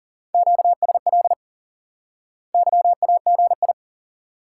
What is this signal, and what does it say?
Morse code, heard as a keyed tone, about 40 wpm, 700 Hz. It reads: QSL YAGI
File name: IQE Course 3 - Lesson 89 - YAGI - Two Elements - 40wpm